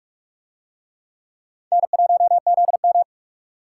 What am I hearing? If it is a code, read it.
Morse code, heard as a keyed tone, about 42 wpm, 700 Hz. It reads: N1ZM